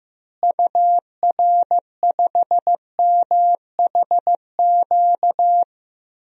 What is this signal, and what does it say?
Morse code, heard as a keyed tone, about 15 wpm, 700 Hz. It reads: UR5MHQ
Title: DXCC-100 Call Signs - 6 Characters - Repeated 3x - 15wpm